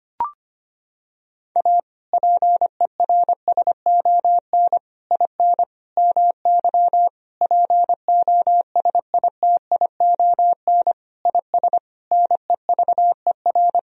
Morse code, heard as a keyed tone, about 25 wpm, 700 Hz. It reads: A PERSON IN MY POSITION IS NEVER
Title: Sentences from Top 1000 Words - Review Part 4 - 25wpm